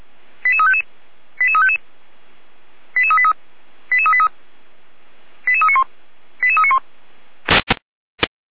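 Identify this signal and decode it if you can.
Five-tone selective call (ZVEI), heard as quick runs of five tones, tones of about 70 ms. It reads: 88355 88355 88383 88383 88391 88391
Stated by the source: ZVEI five-tone selective call received on SDR.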